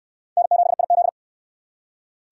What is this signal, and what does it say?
Morse code, heard as a keyed tone, about 60 wpm, 700 Hz. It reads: N7IZ